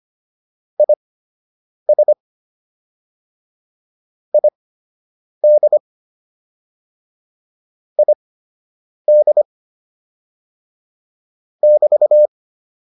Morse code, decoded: IS ID ID <BT>

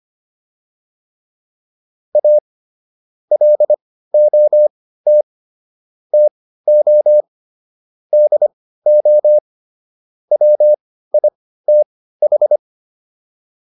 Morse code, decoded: A LOT TO DO WITH

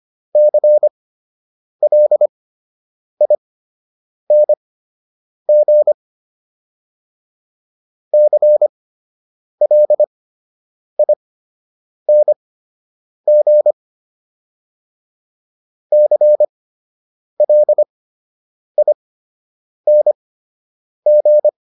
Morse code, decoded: CLING CLING CLING

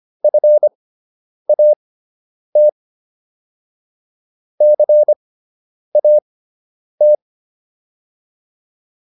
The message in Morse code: FAT CAT

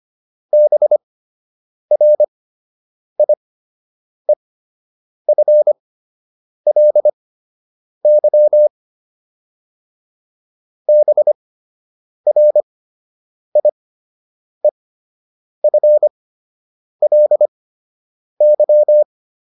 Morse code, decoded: BRIEFLY BRIEFLY

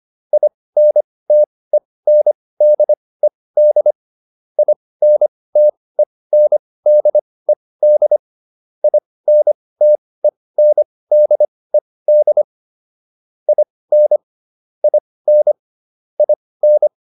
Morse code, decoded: INTENDED INTENDED INTENDED IN IN IN

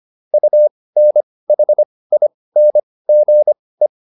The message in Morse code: UNHINGE

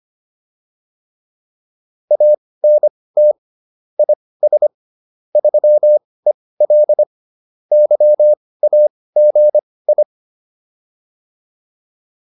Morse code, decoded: ANT IS 3EL YAGI

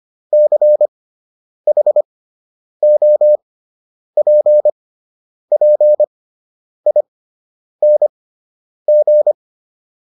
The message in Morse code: CHOPPING

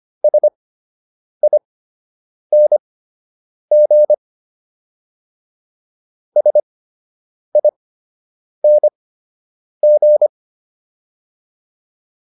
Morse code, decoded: SING SING

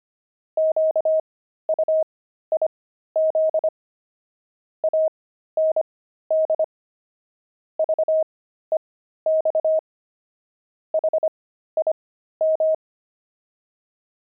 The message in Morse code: QUIZ AND VEX HIM

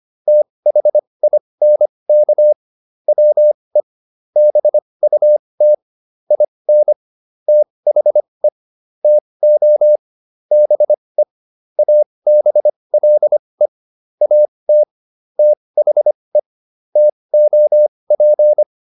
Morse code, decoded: THINK WE BUT IN THE TO BE ABLE AT THE TOP

